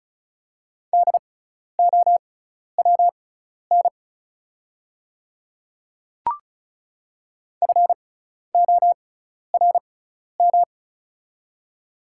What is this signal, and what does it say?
Morse code, heard as a keyed tone, about 36 wpm, spaced as for 12 wpm, 700 Hz. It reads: DOWN FORM
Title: 6x Character Spacing - Top 200 Words - 35wpm